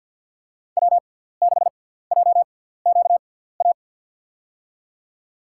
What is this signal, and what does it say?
Morse code, heard as a keyed tone, about 50 wpm, spaced as for 17 wpm, 700 Hz. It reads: W6JQA
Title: USA Call Signs - 6x Character Spacing - 50wpm